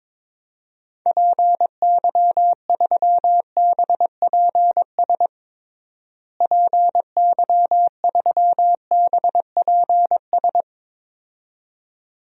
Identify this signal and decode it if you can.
Morse code, heard as a keyed tone, about 22 wpm, 700 Hz. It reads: PY3BPS PY3BPS
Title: DXCC-100 Call Signs - 6 Characters - Repeated 2x - 22wpm